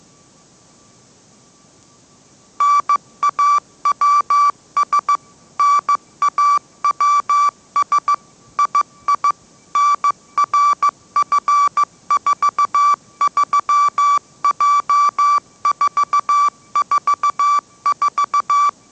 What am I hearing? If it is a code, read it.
Morse code, heard as a keyed tone, about 15 wpm, 1160 Hz. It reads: NAWS NAWS II NRF43J444
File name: mrx / mrx066 16wpm